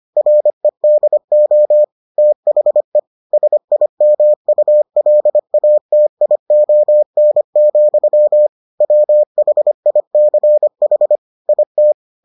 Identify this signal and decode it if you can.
Morse code, heard as a keyed tone, about 25 wpm, 600 Hz. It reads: REDO THE SIMULATION, WHICH IT